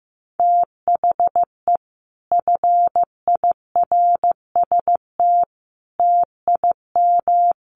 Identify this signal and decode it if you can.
Morse code, heard as a keyed tone, about 15 wpm, 700 Hz. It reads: THE FIRST TIM
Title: Sets of 5 Words Encore 15wpm